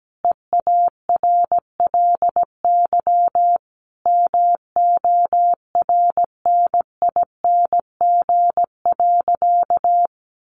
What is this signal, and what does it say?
Morse code, heard as a keyed tone, about 17 wpm, 700 Hz. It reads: EARLY MORNING.
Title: Sentences from Top 1300 Words 17wpm